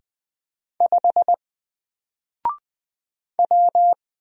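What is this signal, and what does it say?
Morse code, heard as a keyed tone, about 20 wpm, 700 Hz. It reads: 5 W
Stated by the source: Single letter and number Morse drill at 20 wpm, from N4LCD.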